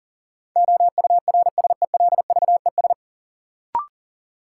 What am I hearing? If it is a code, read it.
Morse code, heard as a keyed tone, about 40 wpm, 700 Hz. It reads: OURSELVES
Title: Sentences from Top 1600 Words - New Word Drill - 40wpm